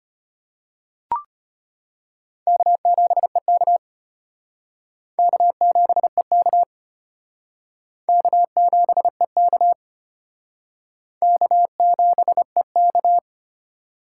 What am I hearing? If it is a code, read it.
Morse code, heard as a keyed tone, about 33 wpm, 700 Hz. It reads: K7EK K7EK K7EK K7EK